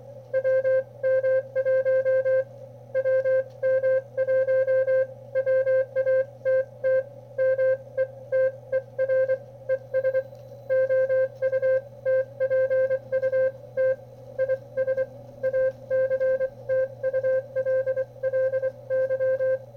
Morse code, heard as a keyed tone, about 24 wpm, spaced as for 21 wpm, 520 Hz. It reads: WM1 WM1 WATT METER ES OUTPUT IS ACTULLY
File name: qrp / wb0jnr at 400mW with ae5x 02-23-2018